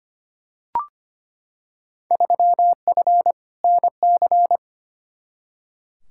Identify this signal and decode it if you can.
Morse code, heard as a keyed tone, about 25 wpm, 700 Hz. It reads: 3F NC
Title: Field Day Exchanges - 25wpm